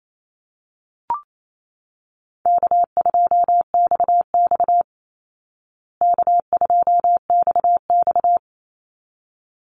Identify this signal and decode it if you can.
Morse code, heard as a keyed tone, about 28 wpm, 700 Hz. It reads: K2XX K2XX
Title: Easy Call Signs Repeated 2x - 28wpm